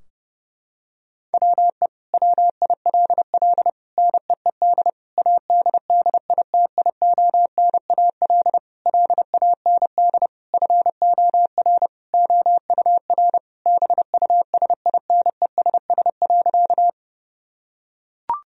Morse code, heard as a keyed tone, about 30 wpm, 700 Hz. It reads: WE WILL NEED ADDITIONAL LAND FOR OUR BUSINESS.
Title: Sentences from Top 1000 Words - Review Part 4 - 30wpm